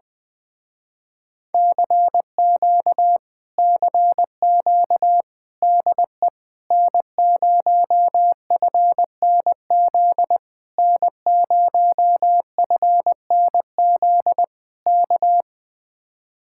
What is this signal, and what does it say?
Morse code, heard as a keyed tone, about 20 wpm, 700 Hz. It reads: CQ CQ DE N0FNZ N0FNZ K
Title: Speed-Racing Common QSO Phrases Encore - 20wpm